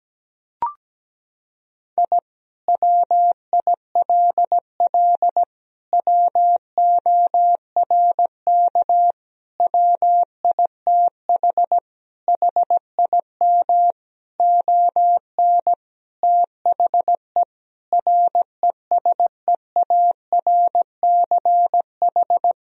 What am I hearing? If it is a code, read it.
Morse code, heard as a keyed tone, about 17 wpm, 700 Hz. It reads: I WILL WORK WITH HIM ON THE RESEARCH
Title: Sentences from Top 1000 Words - Review Part 4 - 17wpm